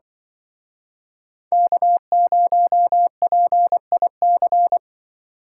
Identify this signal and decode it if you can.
Morse code, heard as a keyed tone, about 25 wpm, 700 Hz. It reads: K0PIC
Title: Speed Racing - USA Call Signs 20wpm